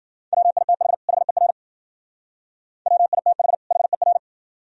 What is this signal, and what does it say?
Morse code, heard as a keyed tone, about 52 wpm, 700 Hz. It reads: WITH HER WITH HER